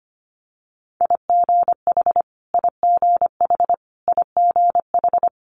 Morse code, heard as a keyed tone, about 25 wpm, 700 Hz. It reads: IGH IGH IGH
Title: Multi-letter Phonemes Repeated 3x 25wpm